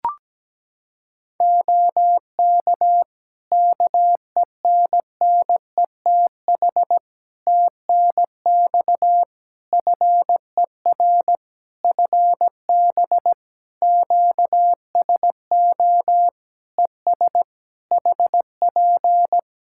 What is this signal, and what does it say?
Morse code, heard as a keyed tone, about 17 wpm, 700 Hz. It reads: OK KENNETH TNX FER FB QSO ES HP
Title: Practice QSO - Exchange 4 - Ending - 17wpm